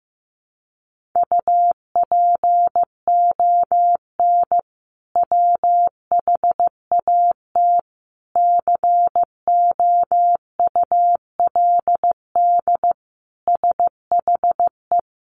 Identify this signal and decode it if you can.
Morse code, heard as a keyed tone, about 15 wpm, 700 Hz. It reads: UPON WHAT COULD SHE
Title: Sentences from Top 1500 Words 15wpm